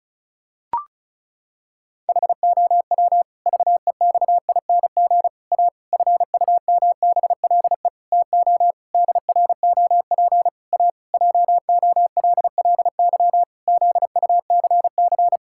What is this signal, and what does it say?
Morse code, heard as a keyed tone, about 36 wpm, 700 Hz. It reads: HOW VEXING A FUMBLE TO DROP A JOLLY ZUCC
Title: Pangram Challenge v2 - 35wpm